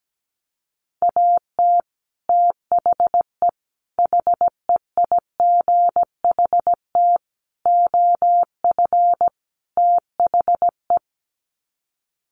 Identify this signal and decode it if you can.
Morse code, heard as a keyed tone, about 17 wpm, 700 Hz. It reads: AT THE HEIGHT OF THE
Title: Sets of 5 Words 17wpm